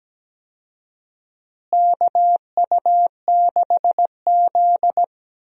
Morse code, heard as a keyed tone, about 17 wpm, 700 Hz. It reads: KU6Z
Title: USA Call Signs - 4 Characters - Repeated 3x - 17wpm